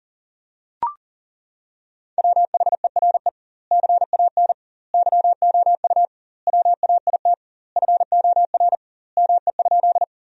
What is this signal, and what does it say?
Morse code, heard as a keyed tone, about 37 wpm, 700 Hz. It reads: WHERE CAN YOU WAIT FOR ME?
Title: 2x Word-Spacing - Sentences from Top 500 Words - 40wpm